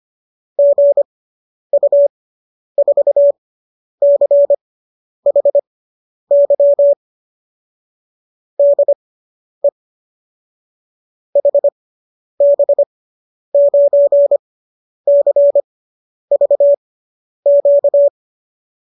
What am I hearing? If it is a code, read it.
Morse code, heard as a keyed tone, about 25 wpm, spaced as for 10 wpm, 570 Hz. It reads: GU4CHY DE HB9CVQ